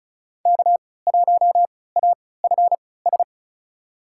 Morse code, heard as a keyed tone, about 34 wpm, spaced as for 20 wpm, 700 Hz. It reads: K1AFS
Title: USA Call Signs - 5 Characters - 3x Character Spacing - 35wpm